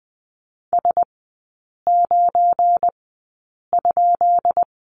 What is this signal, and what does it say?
Morse code, heard as a keyed tone, about 20 wpm, spaced as for 8 wpm, 700 Hz. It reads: S9?
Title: 172 Lesson 28e - Number 9 - Three Characters 20wpm